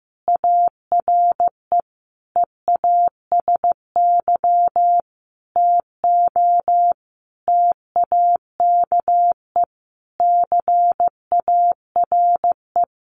Morse code, heard as a keyed tone, about 15 wpm, 700 Hz. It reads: ARE EASY TO TAKE CARE
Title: Additional Easy Phrases 15wpm